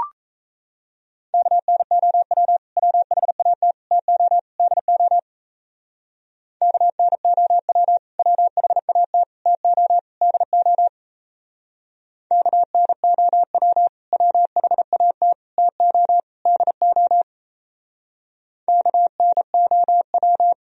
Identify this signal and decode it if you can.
Morse code, heard as a keyed tone, about 36 wpm, 700 Hz. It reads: KNOW WHAT TO DO KNOW WHAT TO DO KNOW WHAT TO DO KNOW